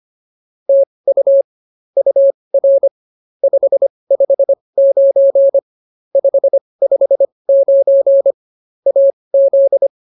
Morse code, heard as a keyed tone, about 24 wpm, spaced as for 20 wpm, 550 Hz. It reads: TU UR 559 559 AZ